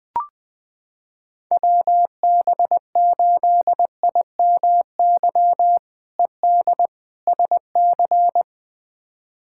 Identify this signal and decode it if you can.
Morse code, heard as a keyed tone, about 20 wpm, 700 Hz. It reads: WB8IMY ED SC
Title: K1USN SST QSO Practice - 20wpm